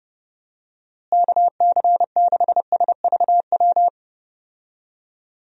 Morse code, heard as a keyed tone, about 32 wpm, 700 Hz. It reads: KC6SVW
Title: USA Call Signs - 6 Characters - Repeated 2x - 30wpm